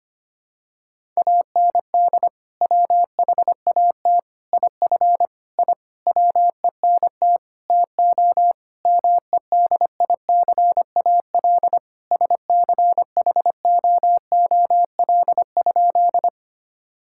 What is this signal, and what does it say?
Morse code, heard as a keyed tone, about 25 wpm, 700 Hz. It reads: AND WHAT IF I WENT TO MEDICAL SCHOOL?
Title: Sentences from Top 1000 Words - Review Part 5 - 25wpm